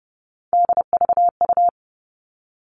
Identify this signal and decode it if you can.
Morse code, heard as a keyed tone, about 30 wpm, 700 Hz. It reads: DVU
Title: All Cut Numbers - 3-Digit 30wpm